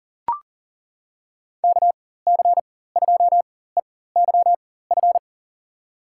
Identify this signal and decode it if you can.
Morse code, heard as a keyed tone, about 40 wpm, spaced as for 19 wpm, 700 Hz. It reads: KC2EYF
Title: USA Call Signs - 4x Character Spacing - 40wpm